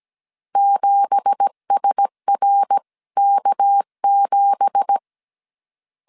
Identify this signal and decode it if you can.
Morse code, heard as a keyed tone, about 17 wpm, 800 Hz. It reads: 7SRK7
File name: CW / DET / 1 9 1500C S C 16wpm 2909